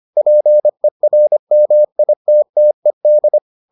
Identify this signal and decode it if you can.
Morse code, heard as a keyed tone, about 25 wpm, 600 Hz. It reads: PERMITTED